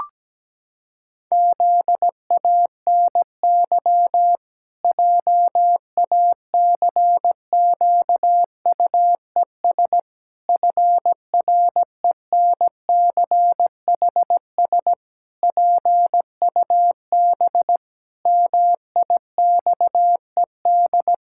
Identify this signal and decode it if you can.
Morse code, heard as a keyed tone, about 17 wpm, 700 Hz. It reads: ZANY JACQUES FRENCHS PUB MIXED
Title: Pangram Challenge v2 - 17wpm